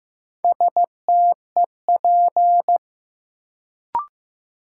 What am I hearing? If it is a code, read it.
Morse code, heard as a keyed tone, about 15 wpm, 700 Hz. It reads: STEP
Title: Sentences from Top 400 Words - New Word Drill - 15wpm